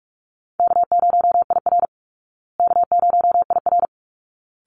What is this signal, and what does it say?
Morse code, heard as a keyed tone, about 44 wpm, 700 Hz. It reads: K0IR K0IR
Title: Easy Call Signs Repeated 2x - 45wpm